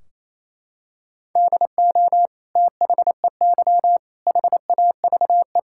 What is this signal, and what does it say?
Morse code, heard as a keyed tone, about 28 wpm, 700 Hz. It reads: DO THEY HAVE
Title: Sentences from Top 1600 Words 28wpm